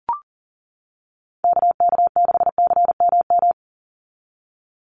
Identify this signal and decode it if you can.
Morse code, heard as a keyed tone, about 40 wpm, 700 Hz. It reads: KK6CMM
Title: USA Call Signs 40wpm